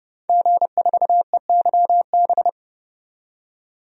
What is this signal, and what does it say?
Morse code, heard as a keyed tone, about 31 wpm, 700 Hz. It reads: G4EYB
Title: DXCC-100 Call Signs - 5 Characters - 30wpm